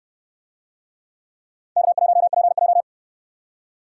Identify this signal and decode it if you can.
Morse code, heard as a keyed tone, about 68 wpm, 700 Hz. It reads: K1PJ